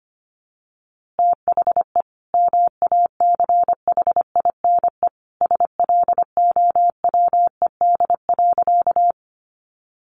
Morse code, heard as a keyed tone, about 25 wpm, 700 Hz. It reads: THE MACHINE SLOWED.
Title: A Sound Of Thunder 25wpm